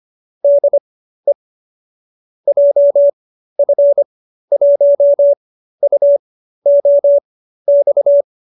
Morse code, heard as a keyed tone, about 25 wpm, spaced as for 13 wpm, 570 Hz. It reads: DE JF1UOX